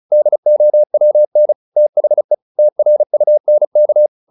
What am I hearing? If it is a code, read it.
Morse code, heard as a keyed tone, about 34 wpm, 600 Hz. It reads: DOWN THE TRUNK